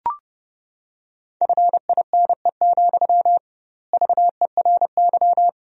Morse code, heard as a keyed tone, about 32 wpm, 700 Hz. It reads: FINE, VERY
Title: 2x Word-Spacing - Sentences from Top 500 Words - 30wpm